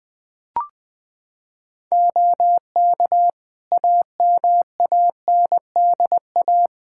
Morse code, heard as a keyed tone, about 20 wpm, 700 Hz. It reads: OK AMANDA
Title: K1USN SST QSO Practice 20wpm